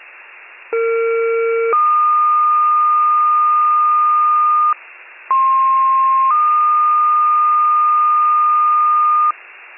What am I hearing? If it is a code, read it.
Two-tone page, heard as a one-second tone, then a three-second tone, about 460 then 1220 Hz, about 1060 then 1220 Hz. It is heard from 0.7 to 9.3 s.